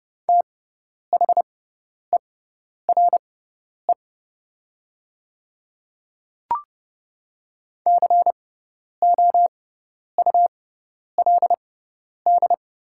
Morse code, heard as a keyed tone, about 30 wpm, spaced as for 10 wpm, 690 Hz. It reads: THERE COULD
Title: 6x Character Spacing - Top 300 Words - 30wpm